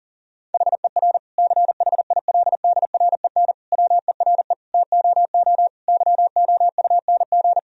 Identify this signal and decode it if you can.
Morse code, heard as a keyed tone, about 40 wpm, 700 Hz. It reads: HER CHILDREN WERE TOO YOUNG
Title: Sentences from Top 1000 Words - Review Part 4 - 40wpm